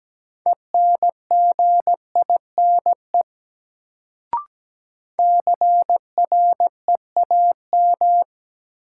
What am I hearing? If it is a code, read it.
Morse code, heard as a keyed tone, about 17 wpm, 700 Hz. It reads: ENGINE CREAM
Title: Sentences from Top 1500 Words - New Word Drill - 17wpm